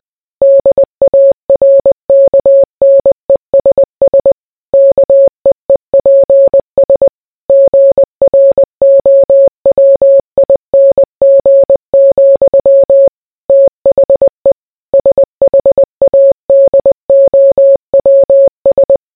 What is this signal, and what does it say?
Morse code, heard as a keyed tone, about 20 wpm, 550 Hz. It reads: DARKNESS KEEPS GROWING, THE SHADOWS